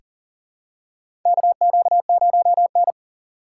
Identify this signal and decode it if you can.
Morse code, heard as a keyed tone, about 40 wpm, 700 Hz. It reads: KQ0N
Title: USA Call Signs - 4 Characters - 6x Character Spacing - 40wpm